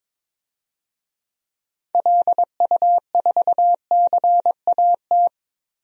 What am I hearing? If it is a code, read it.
Morse code, heard as a keyed tone, about 22 wpm, 700 Hz. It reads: LU4CAT
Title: DXCC-100 Call Signs - 6 Characters - Repeated 2x - 22wpm